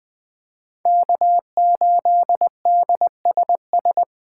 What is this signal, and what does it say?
Morse code, heard as a keyed tone, about 20 wpm, 700 Hz. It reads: K8DSS
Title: K1USN SST QSO Practice - 20wpm